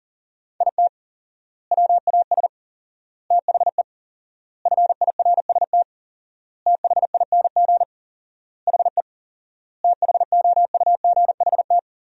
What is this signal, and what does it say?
Morse code, heard as a keyed tone, about 40 wpm, 700 Hz. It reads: IT WAS THE FIRST THING HE THOUGHT
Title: 4x Word-Spacing - Sentences from Top 300 Words - 40wpm